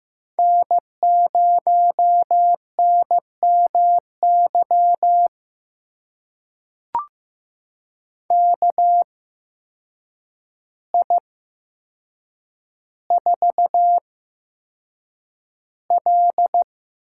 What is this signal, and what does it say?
Morse code, heard as a keyed tone, about 15 wpm, spaced as for 7 wpm, 700 Hz. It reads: N0NMY KI4L
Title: USA Call Signs - 8x Character Spacing - 15wpm